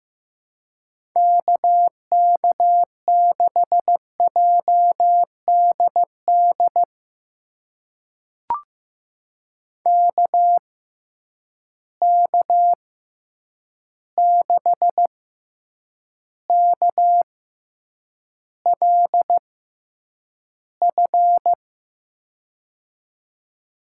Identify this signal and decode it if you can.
Morse code, heard as a keyed tone, about 15 wpm, spaced as for 8 wpm, 700 Hz. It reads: KK6JDD KK6KLF
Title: USA Call Signs - 6 Characters - 6x Character Spacing - 15wpm